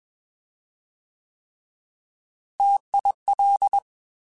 Morse code, heard as a keyed tone, about 21 wpm, 800 Hz. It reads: TIL
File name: Alphabetical / Alphabetical mp3s / Ts @20